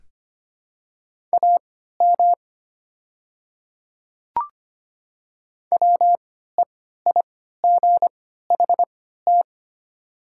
Morse code, heard as a keyed tone, about 25 wpm, spaced as for 14 wpm, 700 Hz. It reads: AM WEIGHT
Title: 3x Character Spacing - Top 500 Words - 25wpm